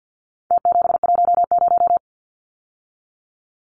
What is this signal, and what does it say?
Morse code, heard as a keyed tone, about 50 wpm, 700 Hz. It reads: T710
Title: Numbers - Contest Style - 4-Digit 50wpm